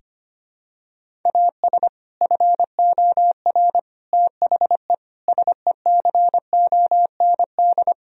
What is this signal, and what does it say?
Morse code, heard as a keyed tone, about 25 wpm, 700 Hz. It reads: AS FOR THE SECOND
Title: Sentences from Top 1000 Words - Review Part 1 - 25wpm